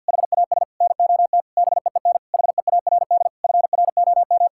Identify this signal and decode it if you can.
Morse code, heard as a keyed tone, about 51 wpm, 700 Hz. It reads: HAS NOT BEEN HEARD FROM
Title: Sentences from Top 1600 Words 50wpm